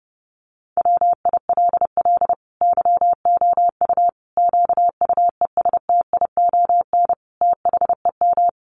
Morse code, heard as a keyed tone, about 30 wpm, 700 Hz. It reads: WILL YOU QUESTION THEM
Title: Sentences from Top 1100 Words 30wpm